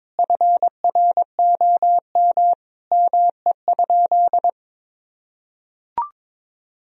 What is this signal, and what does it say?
Morse code, heard as a keyed tone, about 21 wpm, 700 Hz. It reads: FROM ME?
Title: Sentences from Top 1000 Words - Review Part 4 - 22wpm